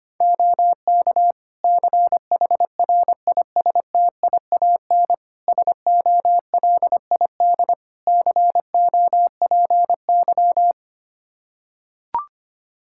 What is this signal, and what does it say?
Morse code, heard as a keyed tone, about 25 wpm, 700 Hz. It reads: OK CHRISTIAN SOLID COPY
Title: Practice QSO - Exchange 3 - More Info - 25wpm